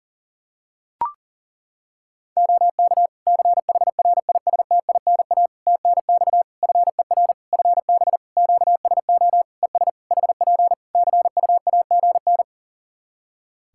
Morse code, heard as a keyed tone, about 40 wpm, 700 Hz. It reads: OK CHRISTINA TNX FER FB QSO ES HP CUAGN
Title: Practice QSO - Exchange 4 - Ending - 40wpm